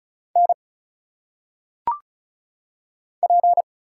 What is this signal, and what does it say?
Morse code, heard as a keyed tone, about 36 wpm, 700 Hz. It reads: N P
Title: Single Letters - Phonetic Pronunciation 35wpm